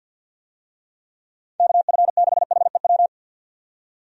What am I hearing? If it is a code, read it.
Morse code, heard as a keyed tone, about 51 wpm, 700 Hz. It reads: KF6HEW
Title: USA Call Signs - 6 Characters - 50wpm